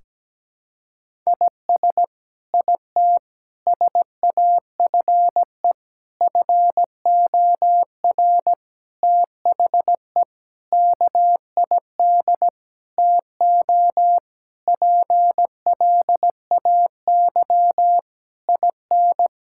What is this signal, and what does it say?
Morse code, heard as a keyed tone, about 17 wpm, 700 Hz. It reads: IS IT SAFE FOR THE KID TO PLAY IN
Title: Sentences from Top 1600 Words 17wpm